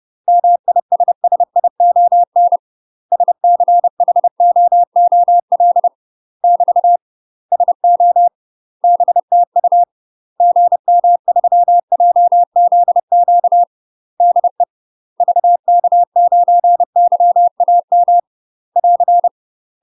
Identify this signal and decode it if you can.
Morse code, heard as a keyed tone, about 30 wpm, 700 Hz. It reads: MISSION SCHOOL <BT> SO BTU GM3JZQ DE VK9YAM <AR>
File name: UK MASTER QSOs TRAINING / 30WPM / UK MASTER QSOs 0027